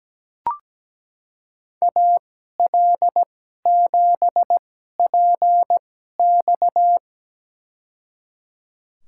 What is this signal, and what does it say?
Morse code, heard as a keyed tone, about 17 wpm, spaced as for 12 wpm, 700 Hz. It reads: AL7PX
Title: USA Call Signs - 2x Character Spacing - 17wpm